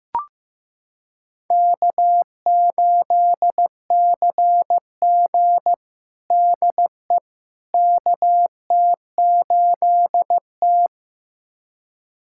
Morse code, heard as a keyed tone, about 15 wpm, 700 Hz. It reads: K8CG DE KT8T
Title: USA Callsign DE Callsign - 4 Characters - 15wpm